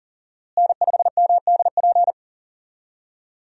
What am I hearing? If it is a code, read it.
Morse code, heard as a keyed tone, about 40 wpm, 700 Hz. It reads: N5MDP
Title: USA Call Signs - 5 Characters - Repeated 3x - 40wpm